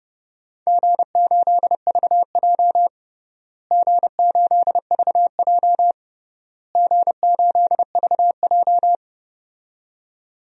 Morse code, heard as a keyed tone, about 32 wpm, 700 Hz. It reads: G8VJ G8VJ G8VJ
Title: DXCC-100 Call Signs - 4 Characters - Repeated 3x - 30wpm